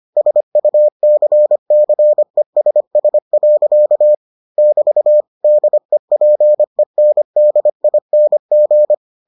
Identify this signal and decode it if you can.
Morse code, heard as a keyed tone, about 25 wpm, 600 Hz. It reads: SUCCESS. <BT> DEPENDING